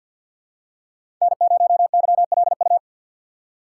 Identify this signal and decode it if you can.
Morse code, heard as a keyed tone, about 50 wpm, 700 Hz. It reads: N0YLU
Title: USA Call Signs - 5 Characters - 8x Character Spacing - 50wpm